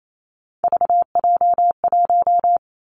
Morse code, heard as a keyed tone, about 28 wpm, 700 Hz. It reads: VJ1
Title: Three Challenging Letter-Number Combinations 28wpm